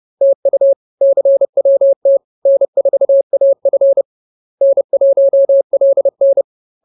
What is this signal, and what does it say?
Morse code, heard as a keyed tone, about 30 wpm, 550 Hz. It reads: TU CWT N4AF N1LN
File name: CWT-Chapter-03